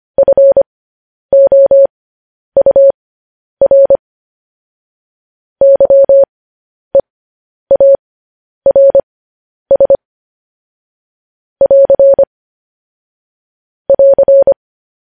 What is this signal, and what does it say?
Morse code, heard as a keyed tone, about 25 wpm, spaced as for 10 wpm, 550 Hz. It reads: FOUR YEARS <AR> <AR>
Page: news-headlines-in-morse-code-at-10-wpm